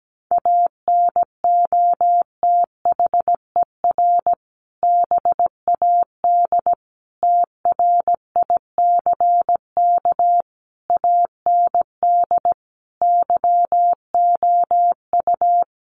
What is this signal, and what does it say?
Morse code, heard as a keyed tone, about 17 wpm, 700 Hz. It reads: ANOTHER BAD TRICK AND YOU
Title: Sentences from Top 1500 Words 17wpm